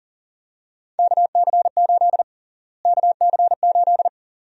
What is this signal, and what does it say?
Morse code, heard as a keyed tone, about 40 wpm, 700 Hz. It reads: KC8 KC8
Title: USA Call Sign Prefixes - Repeated 2x - 40wpm